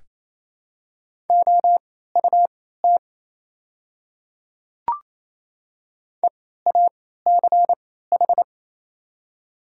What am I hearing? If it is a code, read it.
Morse code, heard as a keyed tone, about 27 wpm, spaced as for 16 wpm, 700 Hz. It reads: OUT EACH
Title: 3x Character Spacing - Top 300 Words - 28wpm